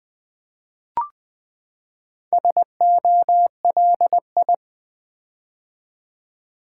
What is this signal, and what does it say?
Morse code, heard as a keyed tone, about 20 wpm, 700 Hz. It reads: SOLI
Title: Build A Word - 4 Letters At A Time - 10-Letter Words - 20wpm